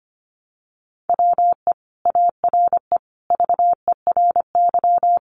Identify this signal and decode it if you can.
Morse code, heard as a keyed tone, about 25 wpm, 700 Hz. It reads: WE ARE VERY
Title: Sentences from Top 800 Words 25wpm